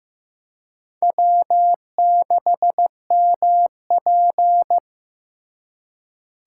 Morse code, heard as a keyed tone, about 18 wpm, 700 Hz. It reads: W6MP
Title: Speed Racing - USA 1x2 Call Signs 15wpm